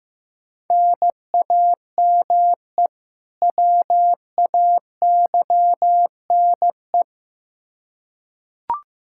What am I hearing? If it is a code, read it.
Morse code, heard as a keyed tone, about 15 wpm, 700 Hz. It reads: NAME WAYNE
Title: Practice QSO - Exchange 1 - Beginning - 15wpm